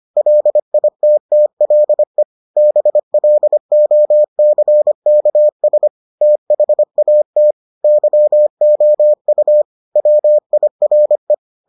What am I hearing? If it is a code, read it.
Morse code, heard as a keyed tone, about 25 wpm, 600 Hz. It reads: LITTLE BLOCKS THAT YOU WIRE